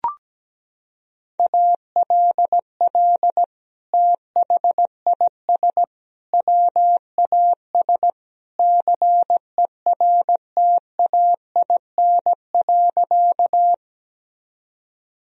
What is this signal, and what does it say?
Morse code, heard as a keyed tone, about 17 wpm, 700 Hz. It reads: ALL THIS WAS CERTAIN.
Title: Sentences from Top 1000 Words - Review Part 2 - 17wpm